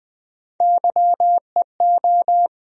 Morse code, heard as a keyed tone, about 20 wpm, 700 Hz. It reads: YEO
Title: USA Call Sign Postfixes - 3 Letters 20wpm